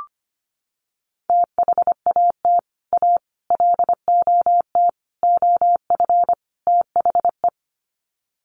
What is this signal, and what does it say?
Morse code, heard as a keyed tone, about 25 wpm, 700 Hz. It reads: THAT A LOT OF THE
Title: Sets of 5 Words Encore2 25wpm